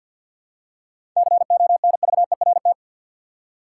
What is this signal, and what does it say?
Morse code, heard as a keyed tone, about 52 wpm, 700 Hz. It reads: CONVERT
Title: Seven Letter Words - 50wpm